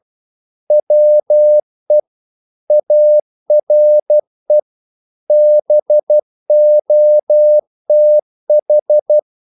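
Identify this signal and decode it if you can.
Morse code, heard as a keyed tone, about 12 wpm, 600 Hz. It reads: WE ARE BOTH
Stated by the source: Morse practice sending English quotations.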